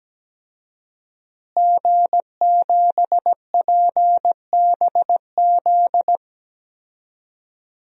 Morse code, heard as a keyed tone, about 17 wpm, 700 Hz. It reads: G7PBZ
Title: DXCC-100 Call Signs - 5 Characters - Repeated 3x - 17wpm